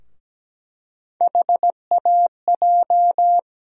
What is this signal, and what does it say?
Morse code, heard as a keyed tone, about 17 wpm, 700 Hz. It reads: HAJ
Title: USA Call Sign Postfixes - 3 Letters 17wpm